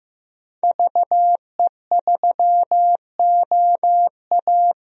Morse code, heard as a keyed tone, about 15 wpm, 700 Hz. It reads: VE3OA
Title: DXCC-100 Call Signs - 5 Characters - 15wpm